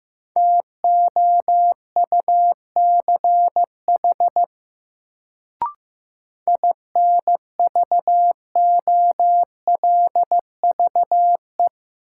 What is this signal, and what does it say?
Morse code, heard as a keyed tone, about 15 wpm, 700 Hz. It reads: TOUCH INVOLVE
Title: Sentences from Top 1200 Words - New Word Drill Alt - 15wpm